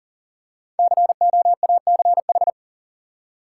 Morse code, heard as a keyed tone, about 40 wpm, 700 Hz. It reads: COACH
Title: Five Letter Words - 40wpm